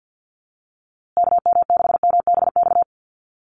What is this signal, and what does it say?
Morse code, heard as a keyed tone, about 50 wpm, 700 Hz. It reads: KM6MBX
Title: USA Call Signs Repeated 3x 50wpm